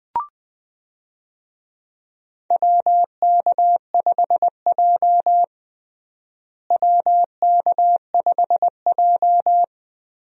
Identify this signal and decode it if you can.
Morse code, heard as a keyed tone, about 20 wpm, 700 Hz. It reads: WK5J WK5J
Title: USA Call Signs - 4 Characters - Repeated 3x - 20wpm